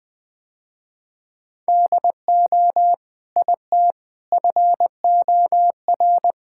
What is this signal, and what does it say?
Morse code, heard as a keyed tone, about 21 wpm, 700 Hz. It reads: DO IT FOR
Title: Sentences with 2-Word Phrases - Spoken Beginning - 20wpm